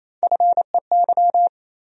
Morse code, heard as a keyed tone, about 28 wpm, 700 Hz. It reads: FEY
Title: USA Call Sign Postfixes - 3 Letters 28wpm